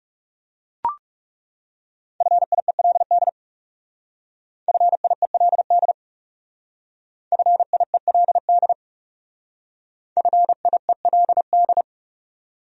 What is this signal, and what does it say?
Morse code, heard as a keyed tone, about 38 wpm, 700 Hz. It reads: FIELD FIELD FIELD FIELD